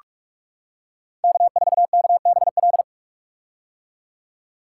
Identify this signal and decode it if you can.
Morse code, heard as a keyed tone, about 46 wpm, 700 Hz. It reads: K4KBL
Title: K1USN SST QSO Practice - 45wpm